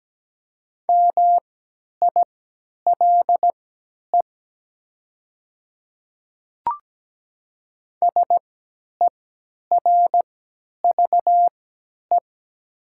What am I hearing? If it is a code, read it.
Morse code, heard as a keyed tone, about 17 wpm, spaced as for 10 wpm, 700 Hz. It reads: MILE SERVE
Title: 3x Character Spacing - Top 500 Words - 17wpm